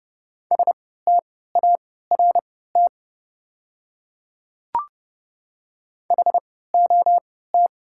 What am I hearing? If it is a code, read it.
Morse code, heard as a keyed tone, about 30 wpm, spaced as for 17 wpm, 690 Hz. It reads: START HOT
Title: 3x Character Spacing - Top 500 Words - 30wpm